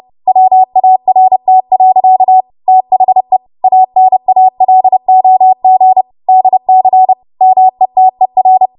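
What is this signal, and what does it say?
Morse code, heard as a keyed tone, about 30 wpm, 750 Hz. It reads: WART. THE ANALOG DC METER
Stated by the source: W1AW Web Code Practice Files, 30 WPM run from ARRL; text from April 2023 QST.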